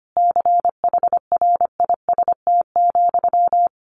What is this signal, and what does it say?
Morse code, heard as a keyed tone, about 25 wpm, 700 Hz. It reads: CHRIST,
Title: A Sound Of Thunder 25wpm